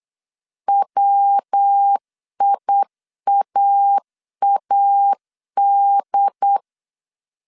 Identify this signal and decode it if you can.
Morse code, heard as a keyed tone, about 8 wpm, 800 Hz. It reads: WIAAD
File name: CW / DET / 1 3 1500C S B 8wpm 4141